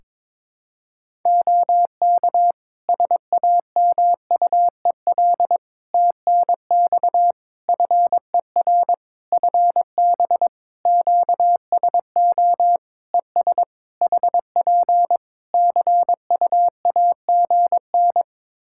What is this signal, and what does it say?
Morse code, heard as a keyed tone, about 22 wpm, 700 Hz. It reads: OK SAMUEL TNX FER FB QSO ES HP CUAGN
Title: K1USN SST QSO Practice 22wpm